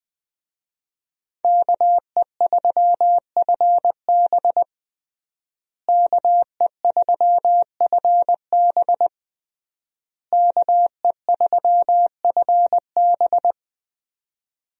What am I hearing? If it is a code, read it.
Morse code, heard as a keyed tone, about 20 wpm, 700 Hz. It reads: KE3FB KE3FB KE3FB
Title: USA Call Signs - 5 Characters - Repeated 3x - 20wpm